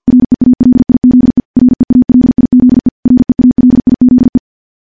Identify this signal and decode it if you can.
Morse code, heard as a keyed tone, about 50 wpm, 250 Hz. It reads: REALIZE REALIZE REALIZE
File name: morse / morse-training-50WPM